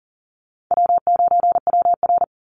Morse code, heard as a keyed tone, about 40 wpm, 700 Hz. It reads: W9WR
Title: Call Signs Repeated 2x - 40wpm